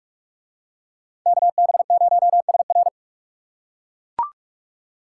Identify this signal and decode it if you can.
Morse code, heard as a keyed tone, about 44 wpm, 700 Hz. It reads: KB0SR